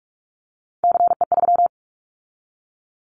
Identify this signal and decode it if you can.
Morse code, heard as a keyed tone, about 44 wpm, 700 Hz. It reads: CE3
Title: Callsign Prefixes for Top 100 DXCC - 45wpm